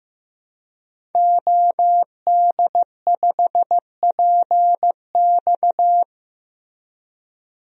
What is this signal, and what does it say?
Morse code, heard as a keyed tone, about 15 wpm, 700 Hz. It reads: OD5PX
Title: DXCC-100 Call Signs - 5 Characters - Repeated 3x - 15wpm